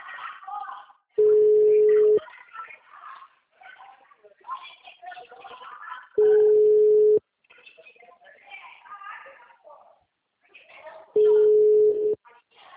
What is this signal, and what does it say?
Ringback tone (Europe, 425 Hz), with speech.